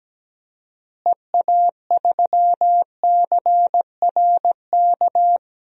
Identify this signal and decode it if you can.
Morse code, heard as a keyed tone, about 17 wpm, 700 Hz. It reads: EA3CRK
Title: DXCC-100 Call Signs - 6 Characters - Repeated 2x - 17wpm